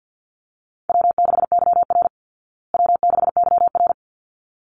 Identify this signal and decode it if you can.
Morse code, heard as a keyed tone, about 49 wpm, 700 Hz. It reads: W6YR W6YR
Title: Easy Call Signs Repeated 2x - 50wpm